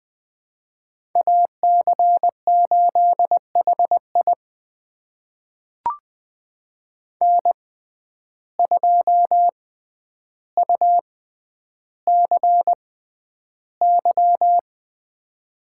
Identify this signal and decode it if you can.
Morse code, heard as a keyed tone, about 20 wpm, spaced as for 10 wpm, 700 Hz. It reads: AC8HI N2UCY
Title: USA Call Signs - 5 Characters - 6x Character Spacing - 20wpm